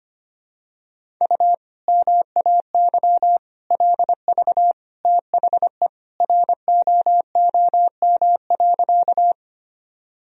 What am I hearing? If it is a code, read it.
Morse code, heard as a keyed tone, about 25 wpm, 700 Hz. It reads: U MAY LV THE ROOM.
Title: Sentences from Top 400 Words and Abbreviations 25wpm